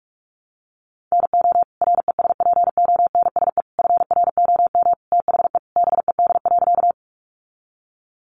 Morse code, heard as a keyed tone, about 44 wpm, 700 Hz. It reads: NO RESPONSE FROM THE BED.
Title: Sentences from Top 1400 Words 45wpm